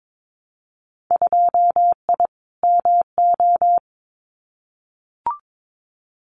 Morse code, heard as a keyed tone, about 22 wpm, 700 Hz. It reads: 2I MO
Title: Winter Field Day Exchanges 22wpm